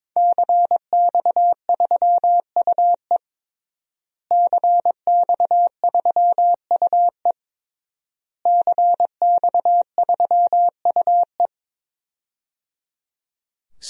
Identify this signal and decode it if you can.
Morse code, heard as a keyed tone, about 23 wpm, 700 Hz. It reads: CX3UE CX3UE CX3UE
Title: DXCC-100 Call Signs - 5 Characters - Repeated 3x - 22wpm